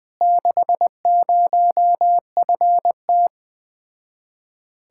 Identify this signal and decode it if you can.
Morse code, heard as a keyed tone, about 20 wpm, 700 Hz. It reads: 60FT